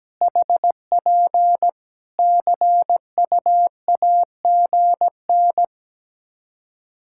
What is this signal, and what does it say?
Morse code, heard as a keyed tone, about 17 wpm, 700 Hz. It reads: HP CUAGN
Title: K1USN SST QSO Practice 17wpm